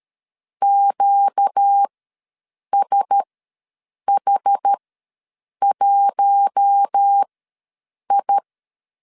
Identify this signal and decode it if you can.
Morse code, heard as a keyed tone, about 13 wpm, spaced as for 7 wpm, 800 Hz. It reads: QSH1I
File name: LEZIONI / 5 LBVC / 5 L ws 12wpm 2958